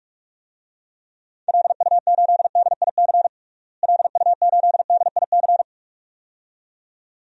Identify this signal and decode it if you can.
Morse code, heard as a keyed tone, about 46 wpm, 700 Hz. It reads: LU8DIC LU8DIC
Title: DXCC-100 Call Signs - 6 Characters - Repeated 2x - 45wpm